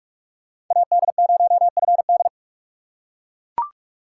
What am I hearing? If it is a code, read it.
Morse code, heard as a keyed tone, about 42 wpm, 700 Hz. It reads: AD0FD
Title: USA Call Signs - 5 Characters - 4x Character Spacing - 45wpm